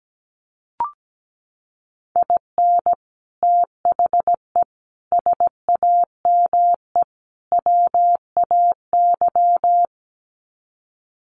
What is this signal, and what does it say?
Morse code, heard as a keyed tone, about 17 wpm, 700 Hz. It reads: IN THE SAME WAY
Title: Sets of 4 Words 17wpm